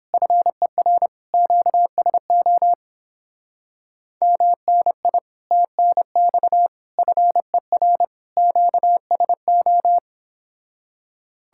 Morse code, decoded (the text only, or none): FER QSO MNI TNX FER QSO